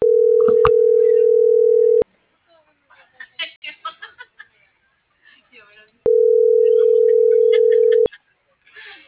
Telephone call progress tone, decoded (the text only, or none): ringback tone